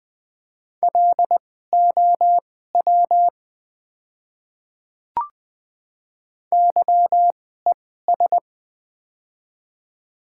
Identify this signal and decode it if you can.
Morse code, heard as a keyed tone, about 20 wpm, spaced as for 14 wpm, 700 Hz. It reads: LOW YES
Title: 2x Character Spacing - Top 500 Words - 20wpm